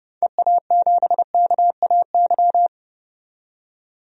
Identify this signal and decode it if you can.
Morse code, heard as a keyed tone, about 30 wpm, 700 Hz. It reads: EA7KAY
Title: DXCC-100 Call Signs - 6 Characters - 30wpm